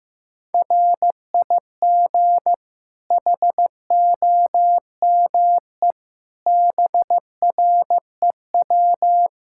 Morse code, heard as a keyed tone, about 15 wpm, 700 Hz. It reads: RIG HOME BREW
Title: Practice QSO - Exchange 2 - Info - 15wpm